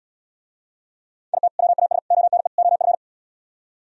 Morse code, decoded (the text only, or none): IT WAS ONE OF